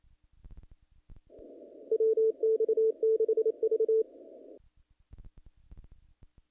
Morse code, heard as a keyed tone, about 27 wpm, 440 Hz. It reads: WX6V
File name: contests / naqp-cw-k5tr-op-k5pi- -1900